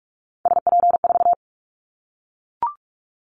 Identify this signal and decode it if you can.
Morse code, heard as a keyed tone, about 44 wpm, 700 Hz. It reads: SP4
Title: Callsign Prefixes for Top 25 DXCC - 45wpm